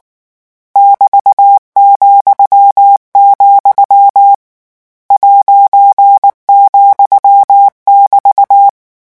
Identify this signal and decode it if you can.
Morse code, heard as a keyed tone, about 19 wpm, 800 Hz. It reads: <BT>,, ',<BT>
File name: morse mp3 / lecon14-vitesse 18